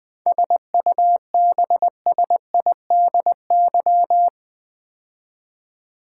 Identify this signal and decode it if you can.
Morse code, heard as a keyed tone, about 20 wpm, 700 Hz. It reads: SUBSIDY
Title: 7 Letter Words - First 3 Letters Given - 20wpm